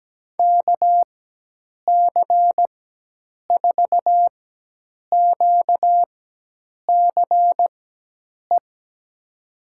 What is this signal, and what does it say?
Morse code, heard as a keyed tone, about 17 wpm, spaced as for 8 wpm, 700 Hz. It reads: KC4QCE
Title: USA Call Signs - 6 Characters - 4x Character Spacing - 17wpm